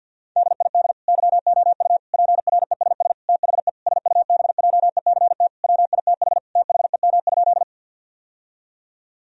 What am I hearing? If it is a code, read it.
Morse code, heard as a keyed tone, about 51 wpm, 700 Hz. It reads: DID YOU PRESS THE SUBJECT WITH THEM?
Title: Sentences from Top 1000 Words - Review Part 1 - 50wpm